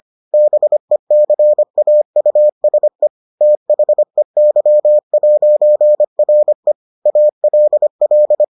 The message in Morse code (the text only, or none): BECAUSE THEY'RE ALL